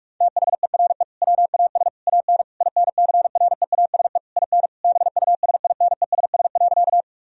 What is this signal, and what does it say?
Morse code, heard as a keyed tone, about 42 wpm, 700 Hz. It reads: THERE WAS AN INCREASE IN BUSINESS.
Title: Sentences from Top 1000 Words - Review Part 4 - 45wpm